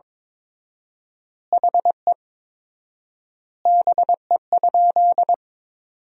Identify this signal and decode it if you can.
Morse code, heard as a keyed tone, about 22 wpm, 700 Hz. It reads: HE BE?
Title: 4x Word-Spacing - Sentences from Top 200 Words - 22wpm